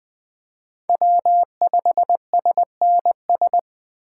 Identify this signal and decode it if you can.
Morse code, heard as a keyed tone, about 20 wpm, 700 Hz. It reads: W5SNS
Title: USA Call Signs - 5 Characters - 4x Character Spacing - 20wpm